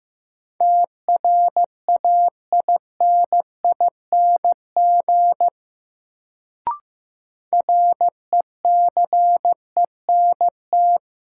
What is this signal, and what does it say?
Morse code, heard as a keyed tone, about 15 wpm, 700 Hz. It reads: TRAINING RECENT
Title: Sentences from Top 1400 Words - New Word Drill Alt - 15wpm